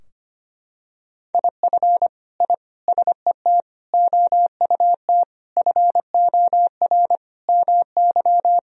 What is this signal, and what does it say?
Morse code, heard as a keyed tone, about 25 wpm, 700 Hz. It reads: IF I SET OUT FOR MY
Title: Sentences from Top 1000 Words - Review Part 2 - 25wpm